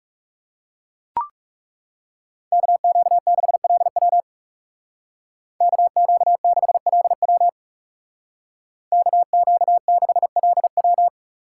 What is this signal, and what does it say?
Morse code, heard as a keyed tone, about 37 wpm, 700 Hz. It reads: KQ6LW KQ6LW KQ6LW
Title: Speed Racing - USA Call Signs 30wpm